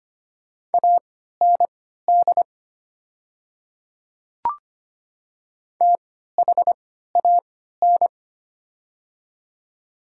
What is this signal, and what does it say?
Morse code, heard as a keyed tone, about 25 wpm, spaced as for 14 wpm, 700 Hz. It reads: AND THAN
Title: 3x Character Spacing - Top 200 Words - 25wpm